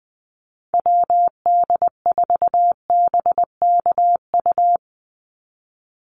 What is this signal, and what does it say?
Morse code, heard as a keyed tone, about 20 wpm, 700 Hz. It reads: WD4BKU
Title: USA Call Signs 20wpm